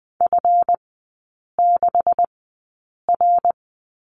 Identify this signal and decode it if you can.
Morse code, heard as a keyed tone, about 20 wpm, spaced as for 8 wpm, 700 Hz. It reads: F6R
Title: 191 Lesson 31e - Letter v - Three Characters 20wpm